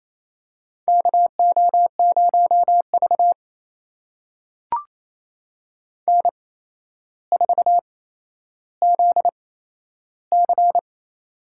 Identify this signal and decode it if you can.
Morse code, heard as a keyed tone, about 28 wpm, spaced as for 12 wpm, 700 Hz. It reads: KO0V N4ZC
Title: USA Call Signs - 4 Characters - 8x Character Spacing - 28wpm